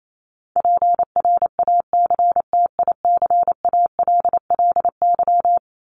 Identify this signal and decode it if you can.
Morse code, heard as a keyed tone, about 28 wpm, 700 Hz. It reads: PRACTICALLY
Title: LY Suffix - Top 100 Words with Base Words 28wpm